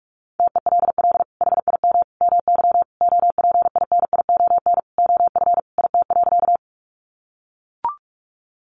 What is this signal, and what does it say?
Morse code, heard as a keyed tone, about 44 wpm, 700 Hz. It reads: TELL HIM MY OPINION OF IT.
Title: Sentences from Top 1200 Words 45wpm